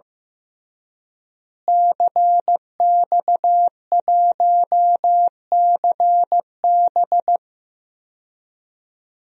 Morse code, heard as a keyed tone, about 16 wpm, 700 Hz. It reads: CX1CB
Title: DXCC-100 Call Signs - 5 Characters - Repeated 3x - 15wpm